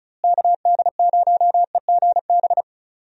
Morse code, heard as a keyed tone, about 36 wpm, 700 Hz. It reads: KD0EGB
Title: USA Call Signs - 6 Characters - Repeated 2x - 35wpm